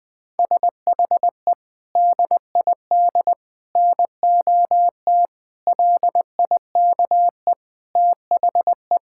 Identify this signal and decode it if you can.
Morse code, heard as a keyed tone, about 20 wpm, 700 Hz. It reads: SHE DID NOT LIKE THE
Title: Sentences from Top 1000 Words - Review Part 5 - 20wpm